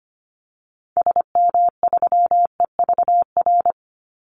Morse code, heard as a keyed tone, about 25 wpm, 700 Hz. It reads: SM3EVR
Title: Call Signs Repeated 2x - 25wpm